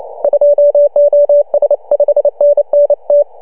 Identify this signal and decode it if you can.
Morse code, heard as a keyed tone, about 29 wpm, 620 Hz. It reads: 2OS 5NN T
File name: CW / HA2OS RHOB 2017 CW